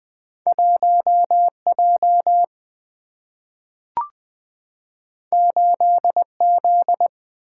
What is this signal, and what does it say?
Morse code, heard as a keyed tone, about 20 wpm, 700 Hz. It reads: 1 J 8 Z
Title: Single-Letter Single-Number 20wpm